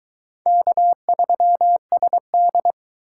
Morse code, heard as a keyed tone, about 23 wpm, 700 Hz. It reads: K3SD